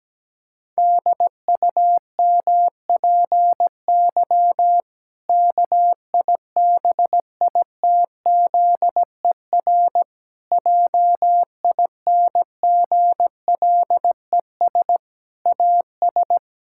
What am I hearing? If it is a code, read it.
Morse code, heard as a keyed tone, about 17 wpm, 700 Hz. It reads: DUMPY KIBITZER JINGLES AS
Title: Pangram Challenge v2 - 17wpm